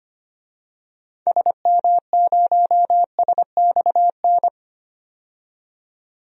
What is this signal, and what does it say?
Morse code, heard as a keyed tone, about 25 wpm, 700 Hz. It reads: SM0SXN
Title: DXCC-100 Call Signs - 6 Characters - 25wpm